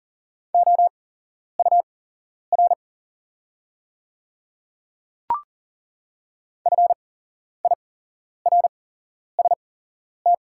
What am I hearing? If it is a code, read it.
Morse code, heard as a keyed tone, about 38 wpm, spaced as for 11 wpm, 690 Hz. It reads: OUR FIRST
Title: 8x Character Spacing - Top 300 Words - 40wpm